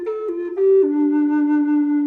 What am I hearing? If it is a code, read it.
There is no code.